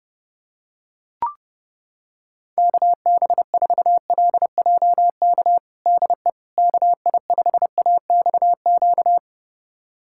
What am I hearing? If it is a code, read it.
Morse code, heard as a keyed tone, about 30 wpm, 700 Hz. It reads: KB4LJK DE KI5AXQ
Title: USA Callsign DE Callsign - 6 Characters - 30wpm